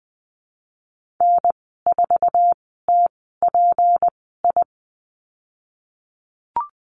Morse code, heard as a keyed tone, about 20 wpm, spaced as for 15 wpm, 700 Hz. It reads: N4TPI
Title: 154 Lesson 25g - Letter p - Call Signs 20wpm